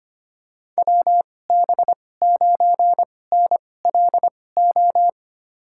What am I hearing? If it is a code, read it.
Morse code, heard as a keyed tone, about 25 wpm, spaced as for 18 wpm, 700 Hz. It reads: WB9NLO
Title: USA Call Signs - 6 Characters - 2x Character Spacing - 25wpm